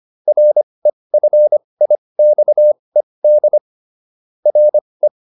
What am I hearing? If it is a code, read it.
Morse code, heard as a keyed tone, about 25 wpm, spaced as for 20 wpm, 600 Hz. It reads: REFIXED RE